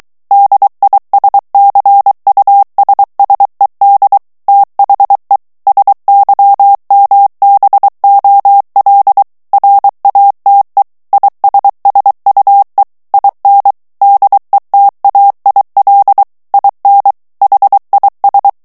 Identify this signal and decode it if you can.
Morse code, heard as a keyed tone, about 23 wpm, 800 Hz. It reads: DISCUSSED THE SYMBOL RATE ISSUE IN DETAIL IN HIS
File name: ANGLAIS texte clair 3 vitesse 22 mots minute